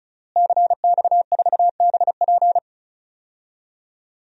Morse code, heard as a keyed tone, about 35 wpm, 700 Hz. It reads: CX4BP
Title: DXCC-100 Call Signs - 5 Characters - Repeated 3x - 35wpm